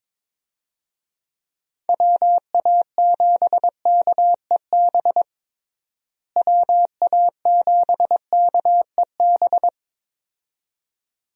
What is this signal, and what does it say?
Morse code, heard as a keyed tone, about 22 wpm, 700 Hz. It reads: WA7KEB WA7KEB
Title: USA Call Signs - 6 Characters - Repeated 2x - 22wpm